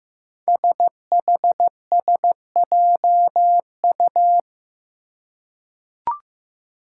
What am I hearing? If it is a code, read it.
Morse code, heard as a keyed tone, about 15 wpm, 700 Hz. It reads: SHSJU
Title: Five Challenging Letter-Number Combinations 15wpm